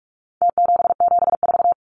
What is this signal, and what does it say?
Morse code, heard as a keyed tone, about 44 wpm, 700 Hz. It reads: T774
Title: Numbers - Contest Style - 4-Digit 45wpm